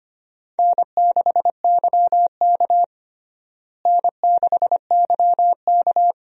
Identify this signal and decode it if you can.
Morse code, heard as a keyed tone, about 25 wpm, 700 Hz. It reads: N6YK N6YK
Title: USA Call Signs - 4 Characters - Repeated 3x - 25wpm